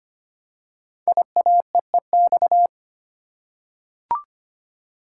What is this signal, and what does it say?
Morse code, heard as a keyed tone, about 25 wpm, 700 Hz. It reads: IAEEX
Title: Code Groups - 5 Letter:Numbers - 25wpm